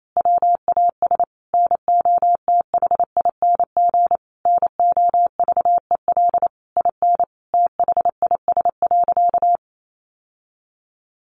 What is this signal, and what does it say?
Morse code, heard as a keyed tone, about 28 wpm, 700 Hz. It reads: WAS NOTHING NOVEL IN THIS.
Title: Sentences from Top 1300 Words 28wpm